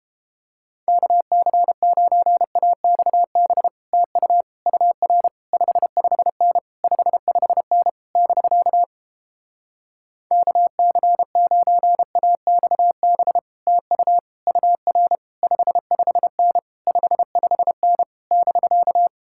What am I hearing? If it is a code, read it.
Morse code, heard as a keyed tone, about 32 wpm, 700 Hz. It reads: KC9AXB TU UR 55N 55N <BK> KC9AXB TU UR 55N 55N <BK>